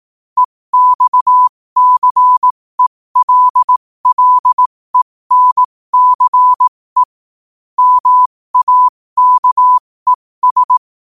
Morse code, heard as a keyed tone, about 18 wpm, spaced as for 15 wpm, 1000 Hz. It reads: EXCELLENCE MAKES
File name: 15wpm